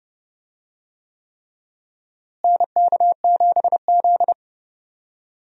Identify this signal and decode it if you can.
Morse code, heard as a keyed tone, about 30 wpm, 700 Hz. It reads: NK7Z
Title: USA Call Signs - 4 Characters - Repeated 3x - 30wpm